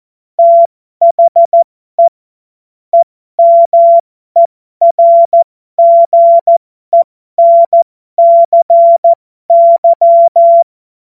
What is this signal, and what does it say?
Morse code, heard as a keyed tone, about 14 wpm, spaced as for 12 wpm, 680 Hz. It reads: THE EMERGENCY
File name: Morsekurs / Witze